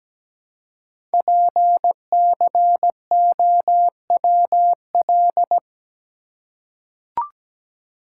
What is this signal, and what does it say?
Morse code, heard as a keyed tone, about 17 wpm, 700 Hz. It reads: PCOWL
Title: Code Groups - 5 Letters - 17wpm